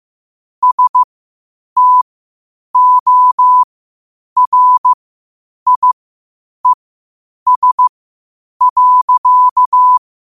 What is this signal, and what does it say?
Morse code, heard as a keyed tone, about 15 wpm, spaced as for 8 wpm, 1000 Hz. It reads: STORIES.
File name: qotdfiles / 8wpm